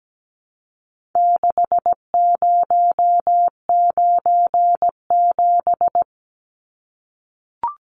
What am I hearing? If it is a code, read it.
Morse code, heard as a keyed tone, about 17 wpm, 700 Hz. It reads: 6097
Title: Numbers - 4-Digit 17wpm